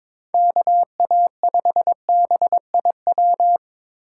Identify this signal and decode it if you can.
Morse code, heard as a keyed tone, about 22 wpm, 700 Hz. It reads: KA5BIW
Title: USA Call Signs - 6 Characters - 4x Character Spacing - 22wpm